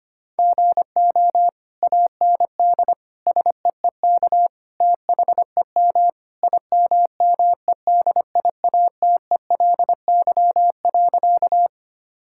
Morse code, heard as a keyed tone, about 25 wpm, 700 Hz. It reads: GO AND SEEK THEM IMMEDIATELY.
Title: Sentences from Top 1600 Words 25wpm